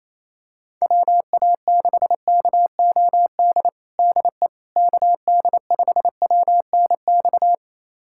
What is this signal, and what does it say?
Morse code, heard as a keyed tone, about 28 wpm, 700 Hz. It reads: WA6KOD DE KD5WNX
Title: USA Callsign DE Callsign - 6 Characters - 28wpm